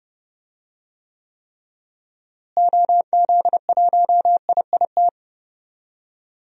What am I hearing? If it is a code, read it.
Morse code, heard as a keyed tone, about 30 wpm, 700 Hz. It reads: OZ1IIT
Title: DXCC-100 Call Signs - 6 Characters - Repeated 3x - 30wpm